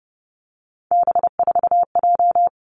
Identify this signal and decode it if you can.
Morse code, heard as a keyed tone, about 30 wpm, 700 Hz. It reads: B4J
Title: Three Challenging Letter-Number Combinations 30wpm